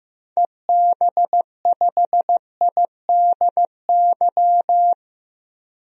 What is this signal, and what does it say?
Morse code, heard as a keyed tone, about 15 wpm, 700 Hz. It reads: EB5IDY
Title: DXCC-100 Call Signs - 6 Characters - 15wpm